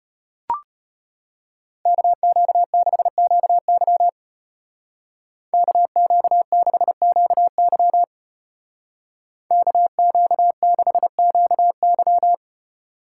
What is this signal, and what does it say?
Morse code, heard as a keyed tone, about 30 wpm, 700 Hz. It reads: KQ6QY KQ6QY KQ6QY
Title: Speed Racing - USA 2x2 Call Signs 25wpm